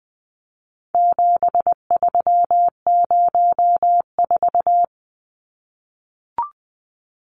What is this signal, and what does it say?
Morse code, heard as a keyed tone, about 20 wpm, 700 Hz. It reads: 7304
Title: Numbers - 4-Digit 20wpm